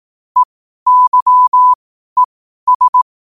Morse code, heard as a keyed tone, about 18 wpm, spaced as for 12 wpm, 1000 Hz. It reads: EYES
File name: qotdfiles / 12wpm